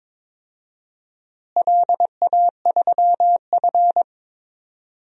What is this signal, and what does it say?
Morse code, heard as a keyed tone, about 22 wpm, 700 Hz. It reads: LA3F
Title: DXCC-100 Call Signs - 4 Characters - Repeated 2x - 22wpm